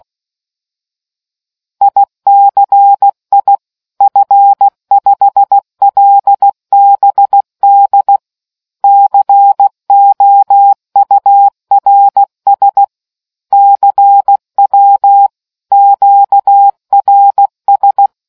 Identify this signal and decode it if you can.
Morse code, heard as a keyed tone, about 16 wpm, 800 Hz. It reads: ICI F5LBD COURS CW QRS